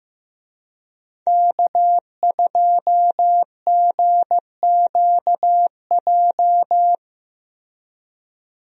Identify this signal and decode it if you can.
Morse code, heard as a keyed tone, about 16 wpm, 700 Hz. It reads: K2GQJ
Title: USA Call Signs - 5 Characters - Repeated 2x - 15wpm